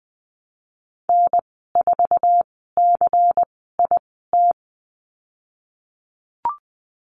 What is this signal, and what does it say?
Morse code, heard as a keyed tone, about 19 wpm, spaced as for 14 wpm, 700 Hz. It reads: N4CIT
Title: 122 Lesson 20g - Number 3 - Call Signs 20wpm